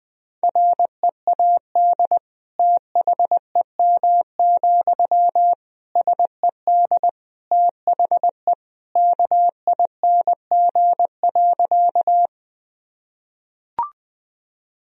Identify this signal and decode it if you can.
Morse code, heard as a keyed tone, about 21 wpm, 700 Hz. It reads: READ THEM, SED THE KING.
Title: Sentences from Top 400 Words and Abbreviations 20wpm